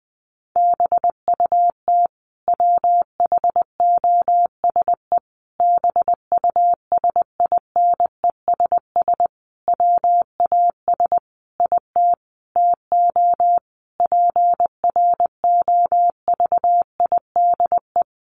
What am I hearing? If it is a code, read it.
Morse code, heard as a keyed tone, about 20 wpm, 700 Hz. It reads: BUT WHOSE BUSINESS WAS IT TO PROVIDE
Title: Sentences from Top 1500 Words 20wpm